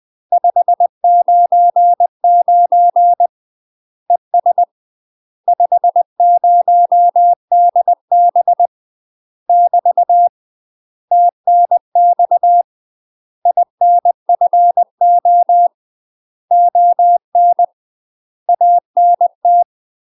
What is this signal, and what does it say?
Morse code, heard as a keyed tone, about 20 wpm, 700 Hz. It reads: 599 ES 50DB <BT> TNX INFO ON ANT
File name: UK MASTER QSOs TRAINING/20WPM